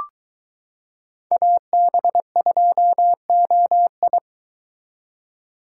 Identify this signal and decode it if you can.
Morse code, heard as a keyed tone, about 23 wpm, 700 Hz. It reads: AB2OI